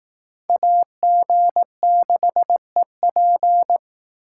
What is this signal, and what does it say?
Morse code, heard as a keyed tone, about 18 wpm, 700 Hz. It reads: AG6EP
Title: Speed Racing - USA 2x2 Call Signs 15wpm